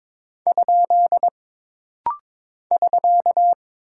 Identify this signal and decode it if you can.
Morse code, heard as a keyed tone, about 22 wpm, 700 Hz. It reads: ? <SK>
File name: practice / 22wpm / Single Punctuation-Prosign - Rapid-Fire - 22wpm